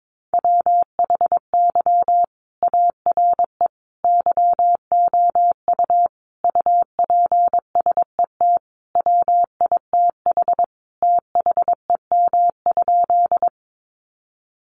Morse code, heard as a keyed tone, about 22 wpm, 700 Hz. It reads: WHY ARE YOU UPSET WITH THEM?
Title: Sentences from Top 800 Words 22wpm